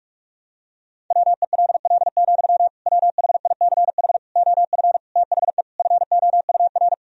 Morse code, heard as a keyed tone, about 42 wpm, 700 Hz. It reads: WELL, WHICH OF THE FOUR
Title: Sentences from Top 1000 Words - Review Part 1 - 45wpm